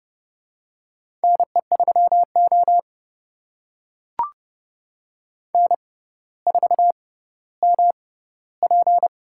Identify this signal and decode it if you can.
Morse code, heard as a keyed tone, about 29 wpm, spaced as for 15 wpm, 700 Hz. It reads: NE3O N4MP
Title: USA Call Signs - 4 Characters - 6x Character Spacing - 30wpm